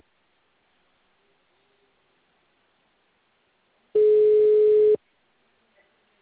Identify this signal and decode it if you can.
Ringback tone (Europe, 425 Hz).
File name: out- -2030- - - .85